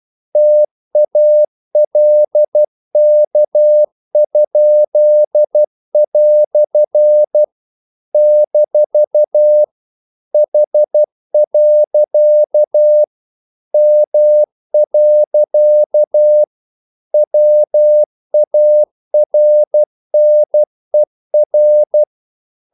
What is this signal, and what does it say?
Morse code, heard as a keyed tone, about 12 wpm, 600 Hz. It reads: TALK?" - H. M. WARNER
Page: English quotations in CW (telegraphy - morse)